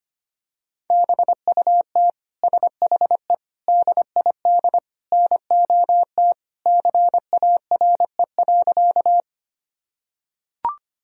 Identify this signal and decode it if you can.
Morse code, heard as a keyed tone, about 25 wpm, 700 Hz. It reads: BUT SHE DID NOT CARE.
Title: Sentences from Top 1000 Words - Review Part 1 - 25wpm